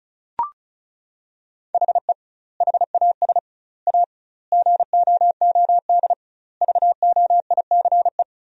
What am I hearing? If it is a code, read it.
Morse code, heard as a keyed tone, about 34 wpm, 700 Hz. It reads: HE HAS A GOOD VOICE
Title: 2x Word-Spacing - Sentences from Top 500 Words - 35wpm